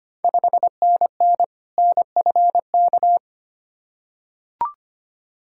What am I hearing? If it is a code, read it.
Morse code, heard as a keyed tone, about 25 wpm, 700 Hz. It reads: 5NN NFK
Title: Ontario QSO Party Exchange 25wpm